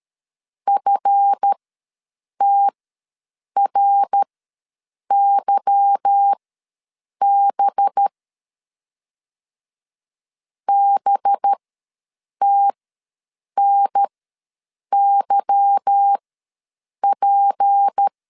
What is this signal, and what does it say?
Morse code, heard as a keyed tone, about 13 wpm, spaced as for 7 wpm, 800 Hz. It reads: FTRYB BTNYP
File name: LEZIONI / 8 YJ45 / 8 Y ws 12wpm 3409